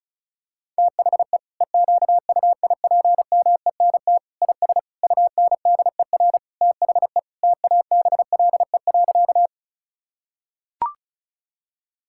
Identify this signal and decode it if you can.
Morse code, heard as a keyed tone, about 36 wpm, 700 Hz. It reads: THE EQUIPMENT IS UNDER THE TABLE.
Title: Sentences from Top 1000 Words - Review Part 5 - 35wpm